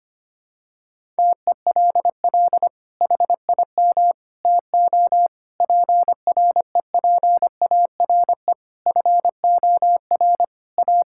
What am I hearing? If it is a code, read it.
Morse code, heard as a keyed tone, about 25 wpm, 700 Hz. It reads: TELL HIM TO PREPARE FOR A
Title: Sentences from Top 1000 Words - Review Part 6 - 25wpm